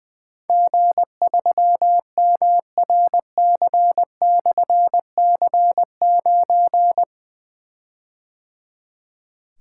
Morse code, heard as a keyed tone, about 20 wpm, 700 Hz. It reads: G3MRC/C9
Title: Difficult Call Signs 20wpm